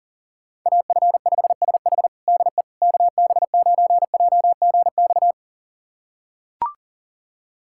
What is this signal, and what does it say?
Morse code, heard as a keyed tone, about 40 wpm, 700 Hz. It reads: AF5SH DE KB9JGX
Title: USA Callsign DE Callsign 40wpm